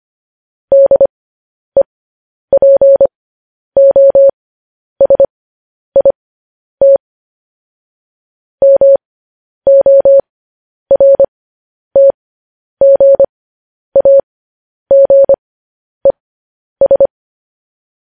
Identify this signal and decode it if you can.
Morse code, heard as a keyed tone, about 25 wpm, spaced as for 10 wpm, 550 Hz. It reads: DEPOSIT MORTGAGES
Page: News Headlines in Morse Code at 10 WPM